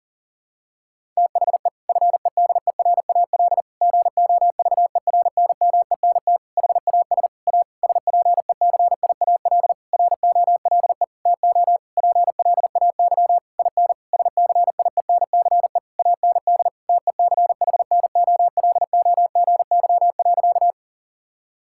Morse code, heard as a keyed tone, about 40 wpm, 700 Hz. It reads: THE FEDERAL GOVERNMENT HAS A SPECIAL ROLE TO PLAY IN SCIENCE AND TECHNOLOGY.
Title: Sentences from Top 1000 Words - Review Part 6 - 40wpm